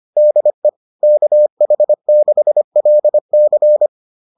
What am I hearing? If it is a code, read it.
Morse code, heard as a keyed tone, about 25 wpm, 600 Hz. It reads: DE KH6LC